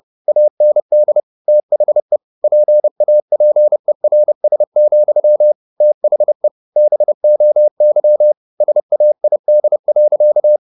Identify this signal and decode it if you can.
Morse code, heard as a keyed tone, about 30 wpm, 600 Hz. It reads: AND THE PAPERS, THE BOY SAID.